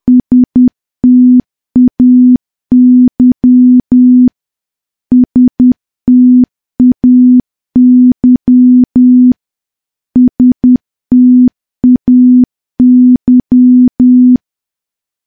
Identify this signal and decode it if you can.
Morse code, heard as a keyed tone, about 10 wpm, 250 Hz. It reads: STAY STAY STAY
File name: morse / morse-training-10WPM